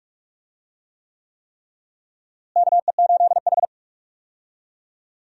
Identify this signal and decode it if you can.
Morse code, heard as a keyed tone, about 46 wpm, 700 Hz. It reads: KE8H
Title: USA Call Signs - 4 Characters - Repeated 3x - 45wpm